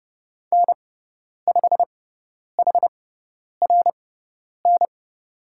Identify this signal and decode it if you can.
Morse code, heard as a keyed tone, about 30 wpm, spaced as for 10 wpm, 700 Hz. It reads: N5HRN
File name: Lesson 17g - Letter u - Call Signs 30wpm